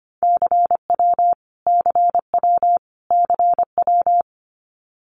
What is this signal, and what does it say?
Morse code, heard as a keyed tone, about 25 wpm, 700 Hz. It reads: CW CW CW
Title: Abbreviations Repeated 3x - 25wpm